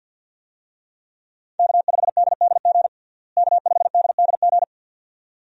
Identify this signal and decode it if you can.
Morse code, heard as a keyed tone, about 51 wpm, 700 Hz. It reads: K5DDG K5DDG
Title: USA Call Signs - 5 Characters - Repeated 2x - 50wpm